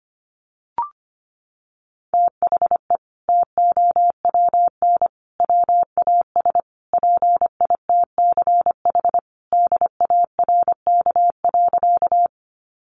Morse code, heard as a keyed tone, about 25 wpm, 700 Hz. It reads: THE TOWN WAS PITCH DARK.
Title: Sentences from Top 1300 Words 25wpm